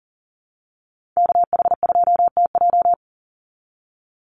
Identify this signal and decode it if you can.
Morse code, heard as a keyed tone, about 40 wpm, 700 Hz. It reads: KH2TJ
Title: NA SOTA Activator Callsigns 40wpm